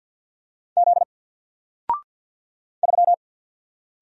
Morse code, heard as a keyed tone, about 50 wpm, 700 Hz. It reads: Z 3
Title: Single Letter-Number Alt 50wpm